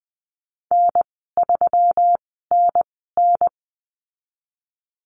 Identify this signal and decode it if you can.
Morse code, heard as a keyed tone, about 20 wpm, spaced as for 15 wpm, 700 Hz. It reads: N3NN
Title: 167 Lesson 27g - Number 7 - Call Signs 20wpm